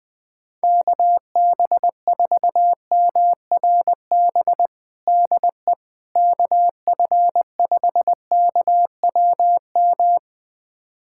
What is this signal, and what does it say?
Morse code, heard as a keyed tone, about 20 wpm, 700 Hz. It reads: KB4MRB DE KF5KWM
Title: USA Callsign DE Callsign 20wpm